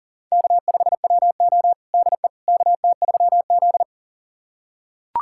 Morse code, heard as a keyed tone, about 40 wpm, 700 Hz. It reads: K5WO DE KT3Z
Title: USA Callsign DE Callsign - 4 Characters - 40wpm